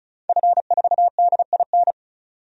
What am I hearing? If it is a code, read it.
Morse code, heard as a keyed tone, about 36 wpm, 700 Hz. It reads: F4DIN
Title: DXCC-100 Call Signs - 5 Characters - 35wpm